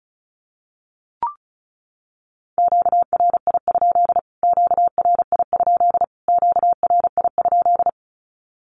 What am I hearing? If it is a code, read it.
Morse code, heard as a keyed tone, about 34 wpm, 700 Hz. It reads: QRI? QRI? QRI?
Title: Q-Codes Repeated 3x 35wpm